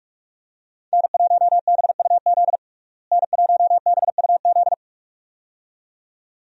Morse code, heard as a keyed tone, about 46 wpm, 700 Hz. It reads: N1BUZ N1BUZ
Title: USA Call Signs - 5 Characters - Repeated 2x - 45wpm